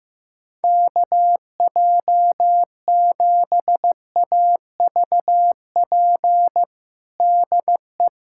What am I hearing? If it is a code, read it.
Morse code, heard as a keyed tone, about 15 wpm, 700 Hz. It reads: KJ7AVP DE
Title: USA Callsign DE Callsign 15wpm